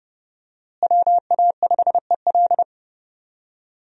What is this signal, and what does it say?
Morse code, heard as a keyed tone, about 30 wpm, 700 Hz. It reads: WA5EL
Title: USA Call Signs - 5 Characters - Repeated 3x - 30wpm